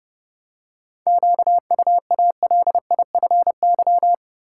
Morse code, heard as a keyed tone, about 31 wpm, 700 Hz. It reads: QUALIFY
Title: Seven Letter Words - 30wpm